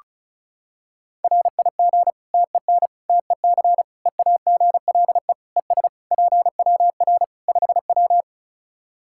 Morse code, heard as a keyed tone, about 36 wpm, 700 Hz. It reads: RIG TEN TEC EAGLE ES PWR 5W
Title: Practice QSO - Exchange 2 - Info - 35wpm